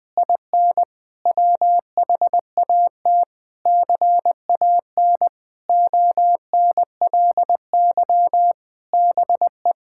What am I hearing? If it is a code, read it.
Morse code, heard as a keyed tone, about 21 wpm, 700 Hz. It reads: IN WHAT CAN ONLY BE
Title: Sentences with 8-Word Phrases - Spoken Beginning - 20wpm